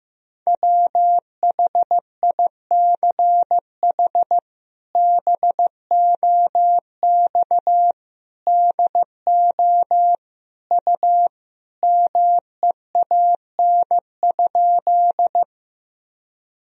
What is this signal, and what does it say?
Morse code, heard as a keyed tone, about 15 wpm, 700 Hz. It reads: WHICH BOX DO U MEAN?
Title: Sentences from Top 500 Words and Abbreviations 15wpm